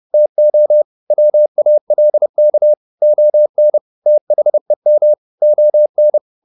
Morse code, heard as a keyed tone, about 30 wpm, 600 Hz. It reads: TO WALK ON THEM ON
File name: the old man and the sea / the old man and the sea 1 30wpm